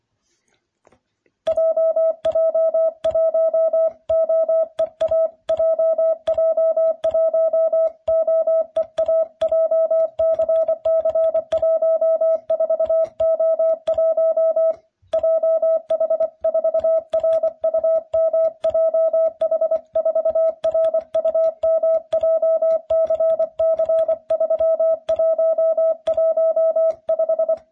Morse code, heard as a keyed tone, about 24 wpm, 660 Hz. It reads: JJ1OEA JJ1OEA JCC14O1 JH4RUM JH4RUM JCC3115